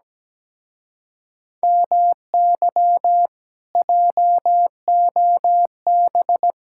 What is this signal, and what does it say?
Morse code, heard as a keyed tone, about 17 wpm, 700 Hz. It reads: MY JOB
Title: Speed-Racing Common QSO Phrases Encore - 17wpm